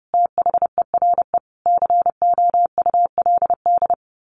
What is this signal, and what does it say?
Morse code, heard as a keyed tone, about 28 wpm, 700 Hz. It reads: THERE COULD
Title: A Sound Of Thunder 30wpm